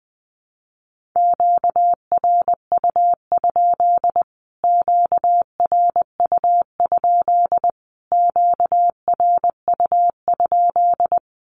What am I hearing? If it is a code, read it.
Morse code, heard as a keyed tone, about 20 wpm, 700 Hz. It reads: QRU? QRU? QRU?
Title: Q-Codes Repeated 3x 20wpm